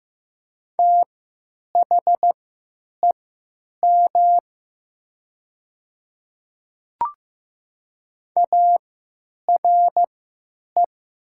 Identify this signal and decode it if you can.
Morse code, heard as a keyed tone, about 15 wpm, spaced as for 9 wpm, 700 Hz. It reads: THEM ARE
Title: 3x Character Spacing - Top 100 Words - 15wpm